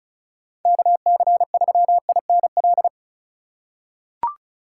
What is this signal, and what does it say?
Morse code, heard as a keyed tone, about 34 wpm, 700 Hz. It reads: KC3INL
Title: USA Call Signs - 6 Characters - 2x Character Spacing - 35wpm